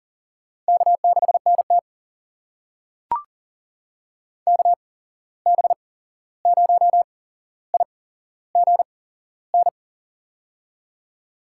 Morse code, heard as a keyed tone, about 42 wpm, spaced as for 15 wpm, 700 Hz. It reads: K6NT KB0IGN
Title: USA Call Signs - 8x Character Spacing - 40wpm